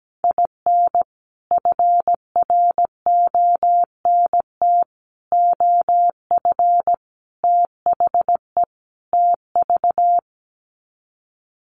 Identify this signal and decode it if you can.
Morse code, heard as a keyed tone, about 17 wpm, 700 Hz. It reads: IN FRONT OF THE TV
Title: Sets of 5 Words Encore2 17wpm